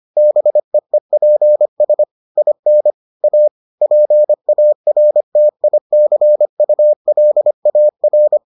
Morse code, heard as a keyed tone, about 25 wpm, 600 Hz. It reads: BEEPS IN A PARTICULAR